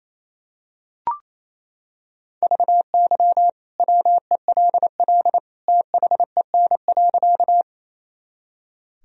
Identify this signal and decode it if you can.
Morse code, heard as a keyed tone, about 28 wpm, 700 Hz. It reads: VY WELL THEN.
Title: Sentences from Top 200 Words and Abbreviations 28wpm